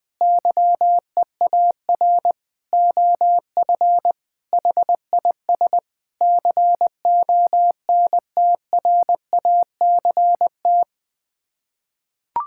Morse code, heard as a keyed tone, about 19 wpm, 700 Hz. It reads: YEAR OF HIS CONTRACT
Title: Sets of 6 Words - 20wpm